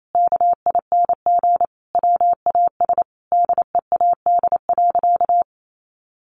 Morse code, heard as a keyed tone, about 28 wpm, 700 Hz. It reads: KING WAS DEAD.
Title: Sentences from Top 1500 Words 28wpm